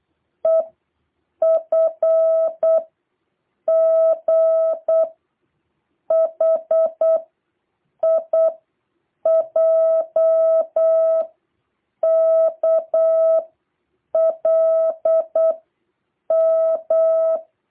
Morse code, heard as a keyed tone, about 8 wpm, 650 Hz. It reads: E F G H I J K L M